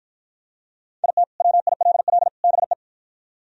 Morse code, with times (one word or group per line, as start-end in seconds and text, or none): 1.0-1.2 IT
1.4-2.3 WILL
2.4-2.7 BE